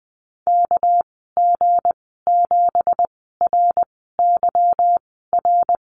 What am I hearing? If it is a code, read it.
Morse code, heard as a keyed tone, about 20 wpm, spaced as for 15 wpm, 700 Hz. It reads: KG7RYR
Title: 245 Lesson 39g - Letter z - Call Signs 20wpm